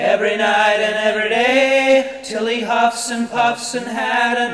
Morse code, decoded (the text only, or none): none